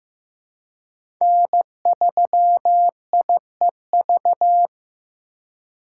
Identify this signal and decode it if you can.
Morse code, heard as a keyed tone, about 15 wpm, 700 Hz. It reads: N3IEV
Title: USA Call Signs - 5 Characters - 6x Character Spacing - 15wpm